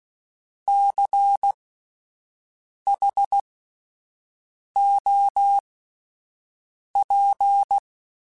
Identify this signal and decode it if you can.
Morse code, heard as a keyed tone, about 16 wpm, spaced as for 5 wpm, 800 Hz. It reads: CHOP
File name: Alphabetical / Alphabetical mp3s / Cs @5